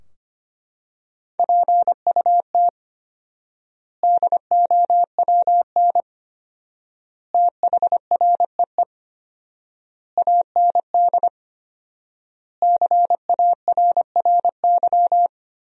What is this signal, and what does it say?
Morse code, heard as a keyed tone, about 25 wpm, 700 Hz. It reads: PUT DOWN THREE AND CARRY
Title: 4x Word-Spacing - Sentences from Top 400 Words - 25wpm